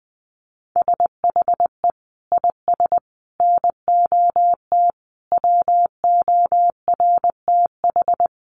Morse code, heard as a keyed tone, about 19 wpm, 700 Hz. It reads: SHE IS NOT WORTH
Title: Sentences from Top 1100 Words 20wpm